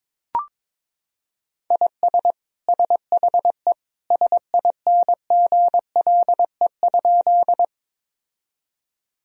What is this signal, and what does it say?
Morse code, heard as a keyed tone, about 21 wpm, 700 Hz. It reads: IS SHE SINGLE?
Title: Sentences from Top 1000 Words - Review Part 4 - 22wpm